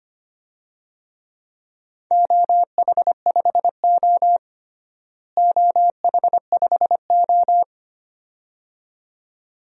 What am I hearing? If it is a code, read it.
Morse code, heard as a keyed tone, about 25 wpm, 700 Hz. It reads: OH5O OH5O
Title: DXCC-100 Call Signs - 4 Characters - Repeated 2x - 25wpm